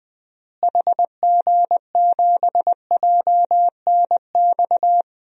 Morse code, heard as a keyed tone, about 21 wpm, 700 Hz. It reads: HG7JNX
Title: DXCC-100 Call Signs - 6 Characters - Repeated 3x - 20wpm